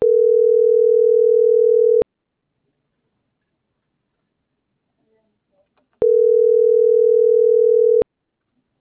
Ringback tone (North America), 440 + 480 Hz.